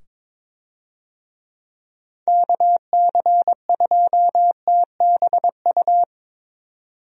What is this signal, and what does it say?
Morse code, heard as a keyed tone, about 22 wpm, 700 Hz. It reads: KC2TBU
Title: USA Call Signs - 6 Characters - 22wpm